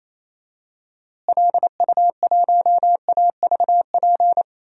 Morse code, heard as a keyed tone, about 28 wpm, 700 Hz. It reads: LU1AVP
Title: DXCC-100 Call Signs - 6 Characters - Repeated 3x - 28wpm